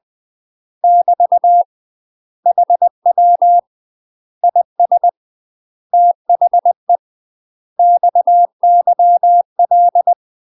Morse code, heard as a keyed tone, about 20 wpm, 700 Hz. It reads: <BT> HW IS THE XYL
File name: UK MASTER QSOs TRAINING / 20WPM / UK MASTER QSOs 0009